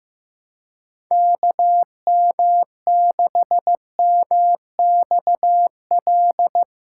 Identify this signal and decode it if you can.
Morse code, heard as a keyed tone, about 16 wpm, 700 Hz. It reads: KM6MXL
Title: USA Callsign DE Callsign 15wpm